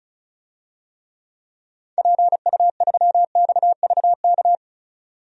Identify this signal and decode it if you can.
Morse code, heard as a keyed tone, about 34 wpm, 700 Hz. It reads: PU3XVK
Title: DXCC-100 Call Signs - 6 Characters - 35wpm